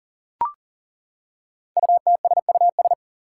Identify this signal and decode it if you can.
Morse code, heard as a keyed tone, about 42 wpm, 700 Hz. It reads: UTSUS
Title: Five Challenging Letter-Number Combinations 40wpm